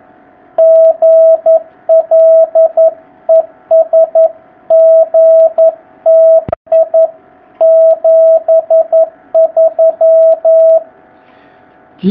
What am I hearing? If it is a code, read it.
Morse code, heard as a keyed tone, about 11 wpm, 660 Hz, with radio noise. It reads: GL ES GTI 73
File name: jh8cbh / 2019cw